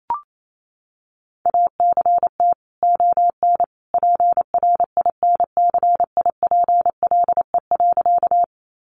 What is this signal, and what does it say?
Morse code, heard as a keyed tone, about 28 wpm, 700 Hz. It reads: ACT ON PRINCIPLE.
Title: Sentences from Top 1400 Words 28wpm